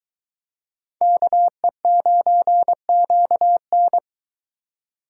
Morse code, heard as a keyed tone, about 22 wpm, 700 Hz. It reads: KE9QN